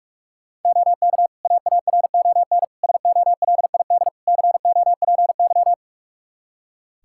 Morse code, heard as a keyed tone, about 46 wpm, 700 Hz. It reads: OK AARON SOLID COPY
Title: K1USN SST QSO Practice 45wpm